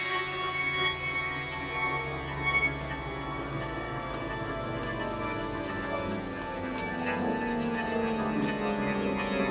Dial-up modem handshake.